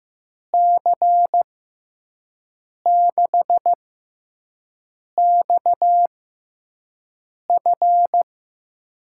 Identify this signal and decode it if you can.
Morse code, heard as a keyed tone, about 15 wpm, spaced as for 5 wpm, 700 Hz. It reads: C6XF
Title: USA Call Signs - 5 Characters - 6x Character Spacing - 15wpm